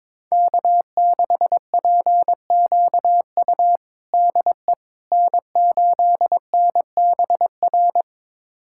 Morse code, heard as a keyed tone, about 22 wpm, 700 Hz. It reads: K6PQU DE N8NBR
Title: USA Callsign DE Callsign - 5 Characters - 22wpm